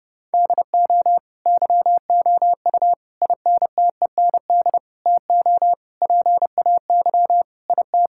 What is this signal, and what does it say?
Morse code, heard as a keyed tone, about 32 wpm, 700 Hz. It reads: DO YOU INTEND TO PAY IT
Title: Sentences from Top 1600 Words 30wpm